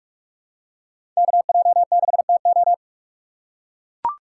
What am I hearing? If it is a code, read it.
Morse code, heard as a keyed tone, about 44 wpm, 700 Hz. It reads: KJ6TO
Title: USA Call Signs - 5 Characters - 3x Character Spacing - 45wpm